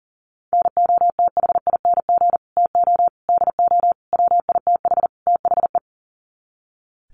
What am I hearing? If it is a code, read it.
Morse code, heard as a keyed tone, about 40 wpm, 700 Hz. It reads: NOTHING TO DO WITH THE
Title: Sets of 5 Words 40wpm